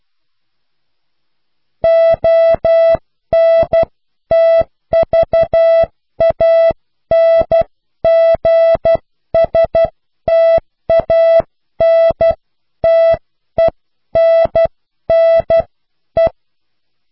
Morse code, heard as a keyed tone, about 12 wpm, spaced as for 10 wpm, 660 Hz. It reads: ONTVANGSTANTENNE